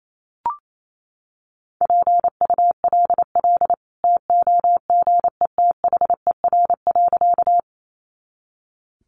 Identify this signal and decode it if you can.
Morse code, heard as a keyed tone, about 28 wpm, 700 Hz. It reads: PULL TOGETHER.
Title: Sentences from Top 400 Words 28wpm